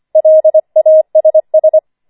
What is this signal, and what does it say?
Morse code, heard as a keyed tone, about 25 wpm, 630 Hz. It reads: LASS
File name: 25wpm